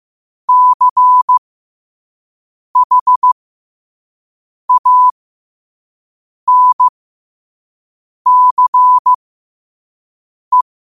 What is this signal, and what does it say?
Morse code, heard as a keyed tone, about 15 wpm, spaced as for 5 wpm, 1000 Hz. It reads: CHANCE
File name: qotdfiles / 5wpm